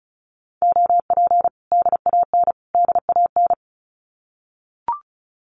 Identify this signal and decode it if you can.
Morse code, heard as a keyed tone, about 34 wpm, 700 Hz. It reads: OP DAN DAN
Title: Common QSO Phrases Encore 35wpm